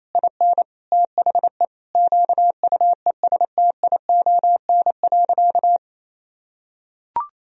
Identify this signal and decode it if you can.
Morse code, heard as a keyed tone, about 28 wpm, 700 Hz. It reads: IN THE QUESTION.
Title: Sentences from Top 1000 Words - Review Part 2 - 28wpm